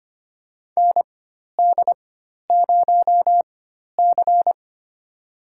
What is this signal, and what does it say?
Morse code, heard as a keyed tone, about 25 wpm, spaced as for 12 wpm, 700 Hz. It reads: ND0C
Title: USA Call Signs - 4 Characters - 4x Character Spacing - 25wpm